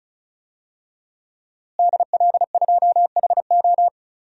Morse code, heard as a keyed tone, about 34 wpm, 700 Hz. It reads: DL2HO
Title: DXCC-100 Call Signs - 5 Characters - Repeated 2x - 35wpm